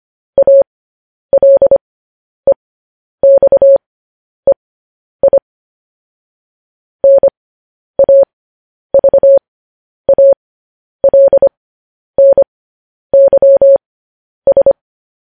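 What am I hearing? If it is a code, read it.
Morse code, heard as a keyed tone, about 25 wpm, spaced as for 10 wpm, 550 Hz. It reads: ALEXEI NAVALNYS
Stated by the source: Morse code news headlines.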